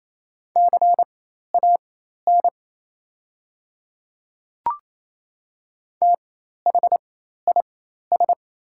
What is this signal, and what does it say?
Morse code, heard as a keyed tone, about 28 wpm, spaced as for 13 wpm, 690 Hz. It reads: CAN THIS
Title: 4x Character Spacing - Top 200 Words - 28wpm